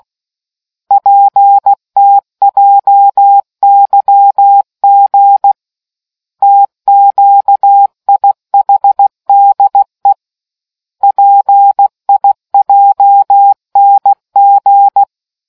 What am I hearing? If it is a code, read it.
Morse code, heard as a keyed tone, about 16 wpm, 800 Hz. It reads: PTJYG TQIHDE PIJNG